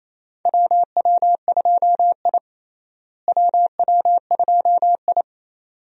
Morse code, heard as a keyed tone, about 28 wpm, 700 Hz. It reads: WW2I WW2I
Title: USA Call Signs - 4 Characters - Repeated 2x - 28wpm